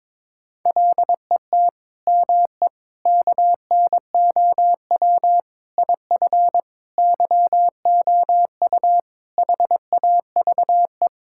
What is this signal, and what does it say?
Morse code, heard as a keyed tone, about 22 wpm, 700 Hz. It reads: LET ME KNOW IF YOU HAVE
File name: Sentences with 7-Word Phrases - 22wpm